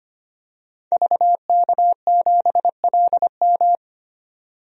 Morse code, heard as a keyed tone, about 25 wpm, 700 Hz. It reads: VK7LM
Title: DXCC-100 Call Signs - 5 Characters - 25wpm